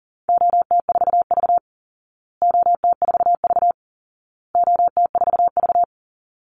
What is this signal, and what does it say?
Morse code, heard as a keyed tone, about 40 wpm, 700 Hz. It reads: OT4V OT4V OT4V
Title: Call Signs Repeated 2x - 40wpm